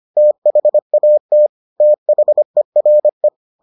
Morse code, heard as a keyed tone, about 25 wpm, 600 Hz. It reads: THAT THERE